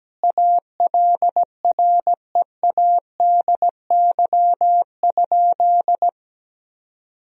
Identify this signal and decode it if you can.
Morse code, heard as a keyed tone, about 17 wpm, 700 Hz. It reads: ALREADY?
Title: Sentences from Top 1000 Words - Review Part 3 - 17wpm